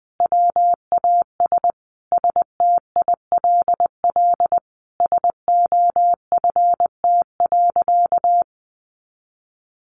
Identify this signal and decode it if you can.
Morse code, heard as a keyed tone, about 19 wpm, 700 Hz. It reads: WAS STILL SOFT.
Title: Sentences from Top 1500 Words 20wpm